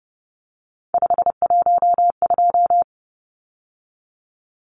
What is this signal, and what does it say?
Morse code, heard as a keyed tone, about 30 wpm, 700 Hz. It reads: 512
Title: Numbers - 3-Digit 30wpm